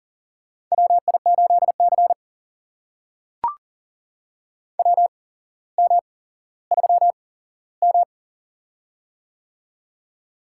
Morse code, heard as a keyed tone, about 44 wpm, spaced as for 17 wpm, 700 Hz. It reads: WI8C WM3M
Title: USA Call Signs - 4 Characters - 8x Character Spacing - 40wpm